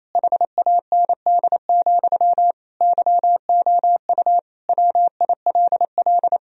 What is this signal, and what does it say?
Morse code, heard as a keyed tone, about 27 wpm, 700 Hz. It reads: HAND, YOU WILL
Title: Sentences from Top 1000 Words - Review Part 3 - 28wpm